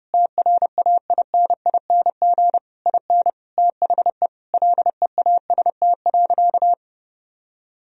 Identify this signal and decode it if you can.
Morse code, heard as a keyed tone, about 30 wpm, 700 Hz. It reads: TRAINING IN THE LEAST.
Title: Sentences from Top 1000 Words - Review Part 4 - 30wpm